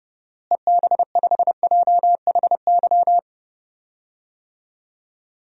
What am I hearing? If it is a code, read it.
Morse code, heard as a keyed tone, about 31 wpm, 700 Hz. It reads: EB5JHY
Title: DXCC-100 Call Signs - 6 Characters - Repeated 3x - 30wpm